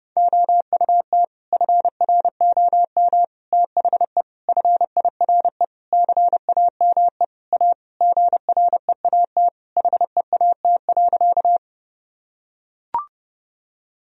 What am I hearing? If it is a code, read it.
Morse code, heard as a keyed tone, about 30 wpm, 700 Hz. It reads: OUT FROM THE FIRE CAME A GREAT HEAT.
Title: Sentences from Top 1000 Words - Review Part 3 - 30wpm